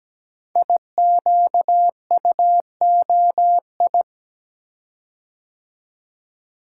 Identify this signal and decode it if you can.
Morse code, heard as a keyed tone, about 17 wpm, 700 Hz. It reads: IQUOI
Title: Code Groups - 5 Letters - 17wpm